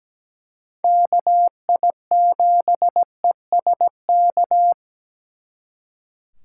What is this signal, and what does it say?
Morse code, heard as a keyed tone, about 20 wpm, 700 Hz. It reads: KI7ESK
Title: Speed Racing - USA 2x3 Call Signs 17wpm